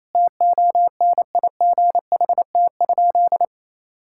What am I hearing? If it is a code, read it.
Morse code, heard as a keyed tone, about 28 wpm, 700 Hz. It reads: TONIGHT?
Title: Sentences from Top 1000 Words - Review Part 5 - 28wpm